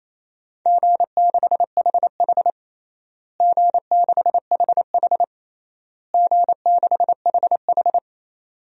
Morse code, heard as a keyed tone, about 28 wpm, 700 Hz. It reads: G6HH G6HH G6HH
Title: DXCC-100 Call Signs - 4 Characters - Repeated 3x - 28wpm